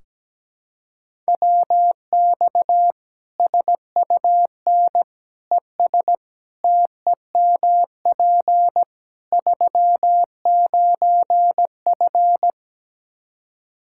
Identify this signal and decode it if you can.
Morse code, heard as a keyed tone, about 17 wpm, 700 Hz. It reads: WX SUN ES TEMP 39F
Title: Practice QSO - Exchange 2 - Info - 17wpm